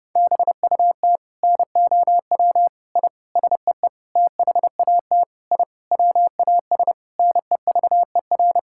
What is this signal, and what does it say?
Morse code, heard as a keyed tone, about 29 wpm, 700 Hz. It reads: BUT NOW I SEE THAT I WAS NEVER
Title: Sentences from Top 1000 Words - Review Part 6 - 30wpm